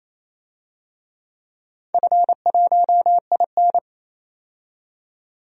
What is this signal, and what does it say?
Morse code, heard as a keyed tone, about 27 wpm, 700 Hz. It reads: F1IN
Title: DXCC-100 Call Signs - 4 Characters - 28wpm